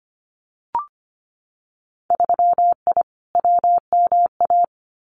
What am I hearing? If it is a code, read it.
Morse code, heard as a keyed tone, about 25 wpm, 700 Hz. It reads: 3I WMA
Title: Winter Field Day Exchanges 25wpm